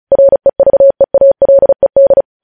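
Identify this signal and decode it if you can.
Morse code, heard as a keyed tone, about 36 wpm, 550 Hz. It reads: REVEALED
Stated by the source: Morse news headlines practice at 35 words per minute.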